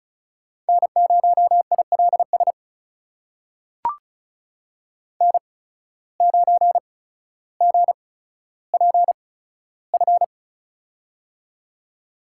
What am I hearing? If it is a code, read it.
Morse code, heard as a keyed tone, about 36 wpm, spaced as for 15 wpm, 700 Hz. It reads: N0ILS N9GPF
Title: USA Call Signs - 5 Characters - 8x Character Spacing - 35wpm